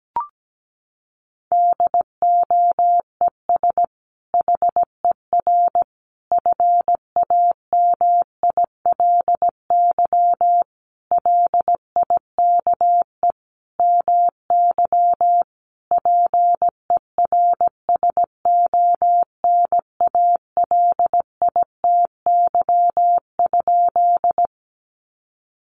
Morse code, heard as a keyed tone, about 17 wpm, 700 Hz. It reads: DOES HER FAMILY LIKE MY PERSONALITY?
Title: Sentences from Top 1400 Words 17wpm